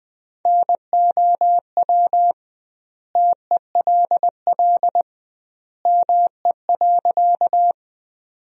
Morse code, heard as a keyed tone, about 20 wpm, 700 Hz. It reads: NOW TELL ME.
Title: 2x Word-Spacing - Sentences from Top 200 Words - 20wpm